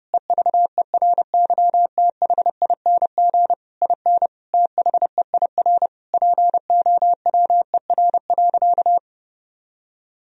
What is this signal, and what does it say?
Morse code, heard as a keyed tone, about 30 wpm, 700 Hz. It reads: EVERYTHING IN THEIR POWER.
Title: Sentences from Top 1000 Words - Review Part 2 - 30wpm